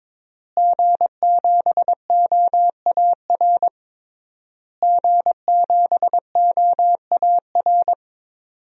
Morse code, heard as a keyed tone, about 22 wpm, 700 Hz. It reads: G7OAR G7OAR
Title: DXCC-100 Call Signs - 5 Characters - Repeated 3x - 22wpm